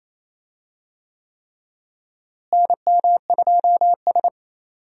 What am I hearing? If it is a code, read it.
Morse code, heard as a keyed tone, about 28 wpm, 700 Hz. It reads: NM2S
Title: USA Call Signs - 4 Characters - Repeated 3x - 28wpm